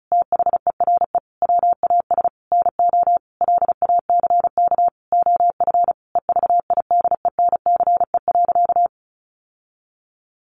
Morse code, heard as a keyed tone, about 34 wpm, 700 Hz. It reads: THERE WAS NO LACK OF EVIDENCE.
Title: Sentences from Top 1300 Words 35wpm